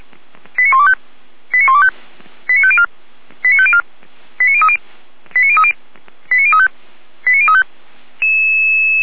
Five-tone selective call (ZVEI), heard as quick runs of five tones, tones of about 80 ms. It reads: 89125 89125 89504 89504 89030 89030 89035 89035